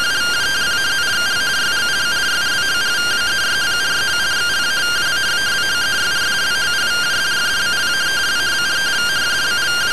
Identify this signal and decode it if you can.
Two-tone teleprinter signal, about 170 Hz shift.